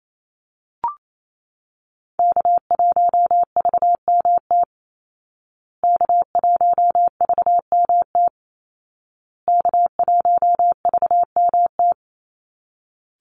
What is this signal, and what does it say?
Morse code, heard as a keyed tone, about 28 wpm, 700 Hz. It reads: K1VMT K1VMT K1VMT
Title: Call Signs Repeated 2x - 28wpm